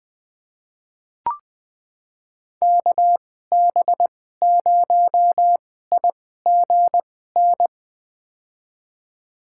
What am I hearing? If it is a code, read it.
Morse code, heard as a keyed tone, about 20 wpm, spaced as for 14 wpm, 700 Hz. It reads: KB0IGN
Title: USA Call Signs - 6 Characters - 2x Character Spacing - 20wpm